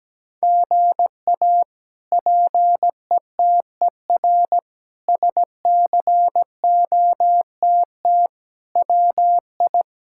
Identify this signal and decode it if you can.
Morse code, heard as a keyed tone, about 17 wpm, 700 Hz. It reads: GA PETER SCOTT WI
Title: K1USN SST QSO Practice - 17wpm